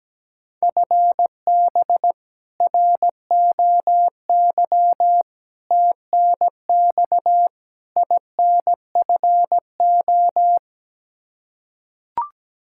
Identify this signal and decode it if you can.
Morse code, heard as a keyed tone, about 17 wpm, 700 Hz. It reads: FB ROY TNX INFO
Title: K1USN SST QSO Practice 17wpm